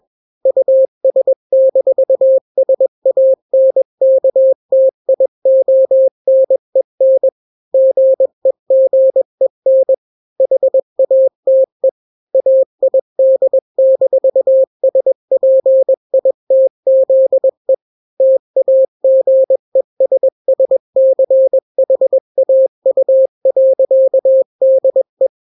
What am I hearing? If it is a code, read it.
Morse code, heard as a keyed tone, about 21 wpm, 530 Hz. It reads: US-SANKTIONEN GEGEN HATE AID-SPITZE TAGESSCHAU.DE
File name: daily / 20wpm